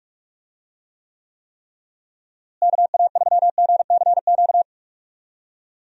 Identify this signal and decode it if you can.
Morse code, heard as a keyed tone, about 43 wpm, 700 Hz. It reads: KA3GCQ
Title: USA Call Signs - 6 Characters - Repeated 3x - 45wpm